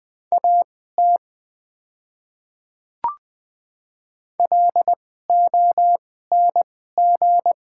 Morse code, heard as a keyed tone, about 20 wpm, spaced as for 14 wpm, 700 Hz. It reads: AT LONG
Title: 2x Character Spacing - Top 100 Words - 20wpm